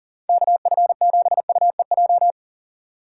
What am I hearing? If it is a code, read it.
Morse code, heard as a keyed tone, about 40 wpm, 700 Hz. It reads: KF7UEJ
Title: USA Call Signs - 6 Characters - 2x Character Spacing - 40wpm